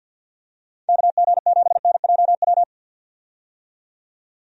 Morse code, heard as a keyed tone, about 50 wpm, 700 Hz. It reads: KG7NJW
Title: USA Call Signs - 6 Characters - 50wpm